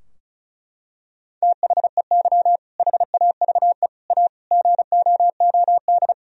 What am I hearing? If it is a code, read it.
Morse code, heard as a keyed tone, about 34 wpm, 700 Hz. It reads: THEY HAVE A GOOD
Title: Sentences from Top 1000 Words - Review Part 4 - 35wpm